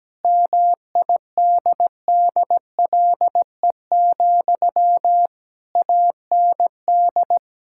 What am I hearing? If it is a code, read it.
Morse code, heard as a keyed tone, about 17 wpm, 700 Hz. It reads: MIDDLE, AND
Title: Sentences from Top 1000 Words - Review Part 6 - 17wpm